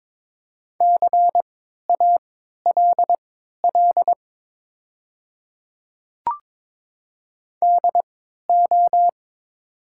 Morse code, heard as a keyed tone, about 22 wpm, spaced as for 13 wpm, 700 Hz. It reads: CALL DO
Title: 3x Character Spacing - Top 100 Words - 22wpm